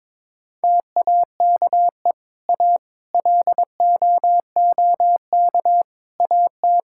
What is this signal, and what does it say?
Morse code, heard as a keyed tone, about 22 wpm, 700 Hz. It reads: TAKE A LOOK AT
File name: Sentences with 7-Word Phrases - 22wpm